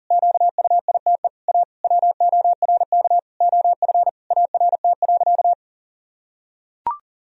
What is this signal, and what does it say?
Morse code, heard as a keyed tone, about 40 wpm, 700 Hz. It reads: QUITE A WORK OF ART.
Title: Sentences from Top 1000 Words - Review Part 3 - 40wpm